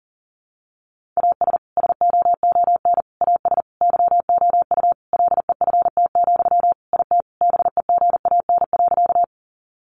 Morse code, heard as a keyed tone, about 40 wpm, 700 Hz. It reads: AS SOON AS YOU LEFT, IT BEGAN.
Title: Sentences from Top 400 Words 40wpm